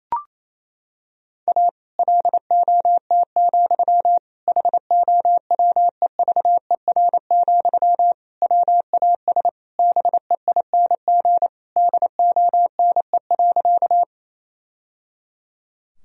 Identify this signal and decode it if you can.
Morse code, heard as a keyed tone, about 28 wpm, 700 Hz. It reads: A LOT, HOWEVER, WAS BEING DONE.
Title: Sentences from Top 1000 Words - Review Part 3 - 28wpm